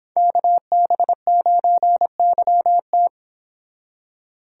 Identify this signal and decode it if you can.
Morse code, heard as a keyed tone, about 26 wpm, 700 Hz. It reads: KB9YT